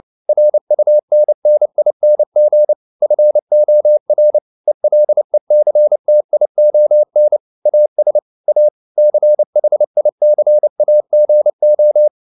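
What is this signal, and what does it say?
Morse code, heard as a keyed tone, about 29 wpm, 600 Hz. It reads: RUNNING FOR ELECTION AS A CHICAGO
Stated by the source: Morse code at 29 wpm, sending Ken Follett's novel Never in English, from CW med Gnister.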